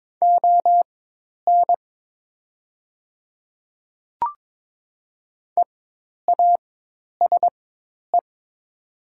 Morse code, heard as a keyed tone, about 22 wpm, spaced as for 10 wpm, 700 Hz. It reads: ON EASE
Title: 4x Character Spacing - Top 300 Words - 22wpm